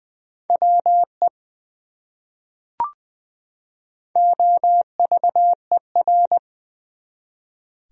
Morse code, heard as a keyed tone, about 21 wpm, 700 Hz. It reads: WE OVER
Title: Top 100 Words 20wpm